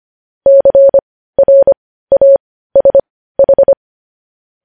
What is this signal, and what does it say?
Morse code, heard as a keyed tone, about 25 wpm, spaced as for 15 wpm, 550 Hz. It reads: CRASH
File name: podcast / News headlines-1x-25x15